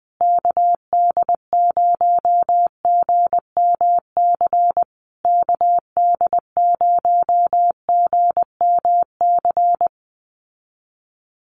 Morse code, heard as a keyed tone, about 20 wpm, 700 Hz. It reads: KD0GMC KD0GMC
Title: USA Call Signs Repeated 3x 20wpm